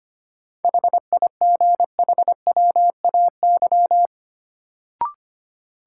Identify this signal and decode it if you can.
Morse code, heard as a keyed tone, about 25 wpm, 700 Hz. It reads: HIGHWAY
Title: Sentences from Top 1400 Words - New Word Drill - 25wpm